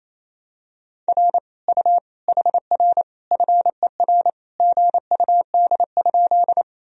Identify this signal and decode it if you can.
Morse code, heard as a keyed tone, about 28 wpm, 700 Hz. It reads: R U HR FER GUD?
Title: Sentences from Top 200 Words and Abbreviations 28wpm